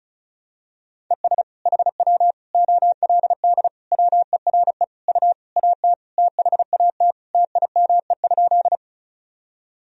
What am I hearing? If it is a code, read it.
Morse code, heard as a keyed tone, about 34 wpm, 690 Hz. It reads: ES HW OLD WERE U AT THAT TIME?
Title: Sentences from Top 200 Words and Abbreviations 35wpm